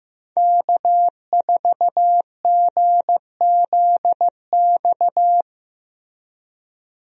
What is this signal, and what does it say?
Morse code, heard as a keyed tone, about 15 wpm, 700 Hz. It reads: K4GZX
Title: USA Call Signs - 5 Characters - Repeated 2x - 15wpm